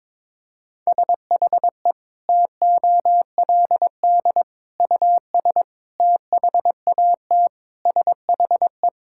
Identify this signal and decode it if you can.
Morse code, heard as a keyed tone, about 22 wpm, 700 Hz. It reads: SHE TOLD US THAT SHE
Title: Sentences from Top 1000 Words - Review Part 6 - 22wpm